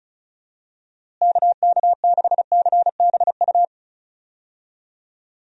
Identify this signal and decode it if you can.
Morse code, heard as a keyed tone, about 36 wpm, 700 Hz. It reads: KK6CBU
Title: USA Call Signs - 6 Characters - Repeated 3x - 35wpm